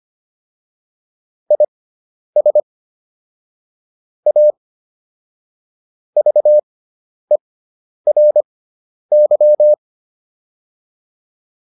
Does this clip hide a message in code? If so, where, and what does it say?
Yes: Morse code from 1.5 to 9.7 s, heard as a keyed tone, about 25 wpm, spaced as for 10 wpm, 600 Hz. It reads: IS A VERY